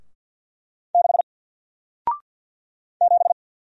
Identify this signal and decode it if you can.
Morse code, heard as a keyed tone, about 50 wpm, 700 Hz. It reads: 6 7
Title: Single Letter-Number Alt 50wpm